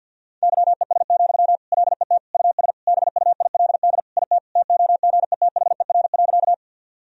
Morse code, heard as a keyed tone, about 52 wpm, 700 Hz. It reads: YES, LET US BUILD IT TOGETHER.
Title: Sentences from Top 1000 Words - Review Part 1 - 50wpm